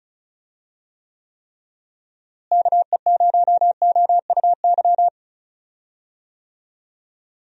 Morse code, heard as a keyed tone, about 34 wpm, 700 Hz. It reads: KE0OUY
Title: USA Call Signs - 6 Characters - Repeated 3x - 35wpm